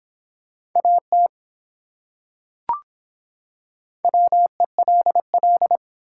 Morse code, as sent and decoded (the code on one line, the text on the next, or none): .- - / .-- . .-.. .-..
AT WELL